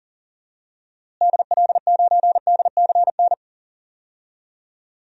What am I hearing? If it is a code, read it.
Morse code, heard as a keyed tone, about 40 wpm, 700 Hz. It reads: DL9DCN
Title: DXCC-100 Call Signs - 6 Characters - Repeated 3x - 40wpm